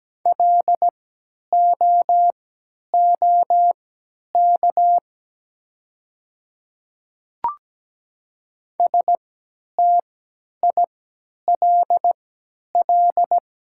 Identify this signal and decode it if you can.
Morse code, heard as a keyed tone, about 17 wpm, spaced as for 10 wpm, 700 Hz. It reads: LOOK STILL
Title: 3x Character Spacing - Top 300 Words - 17wpm